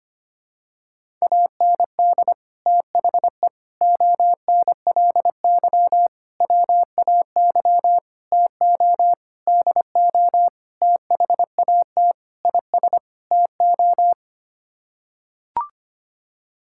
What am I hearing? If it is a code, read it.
Morse code, heard as a keyed tone, about 25 wpm, 700 Hz. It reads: AND THE ONLY WAY TO DO THAT IS TO
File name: Sets of 9 Words - 25wpm